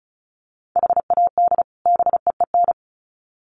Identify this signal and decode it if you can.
Morse code, heard as a keyed tone, about 34 wpm, 700 Hz. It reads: HAD BEEN
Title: Sets of 2 Words 35wpm